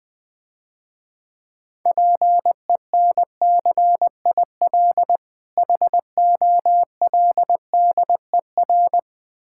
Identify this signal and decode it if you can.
Morse code, heard as a keyed tone, about 21 wpm, 700 Hz. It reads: PENCIL HOLDER
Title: Easy Words - Speed-Racing with Hints - 20wpm